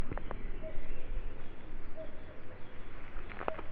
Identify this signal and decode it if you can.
There is no code.